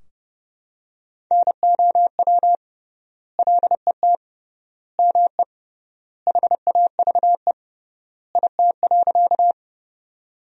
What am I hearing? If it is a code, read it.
Morse code, heard as a keyed tone, about 30 wpm, 700 Hz. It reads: NOW LET ME HAVE IT.
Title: 3x Word-Spacing - Sentences from Top 300 Words - 30wpm